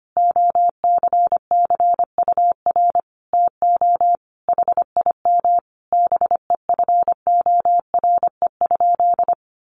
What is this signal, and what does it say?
Morse code, heard as a keyed tone, about 25 wpm, 700 Hz. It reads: OCCUR TO HIM BEFORE?
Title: Sentences from Top 900 Words 25wpm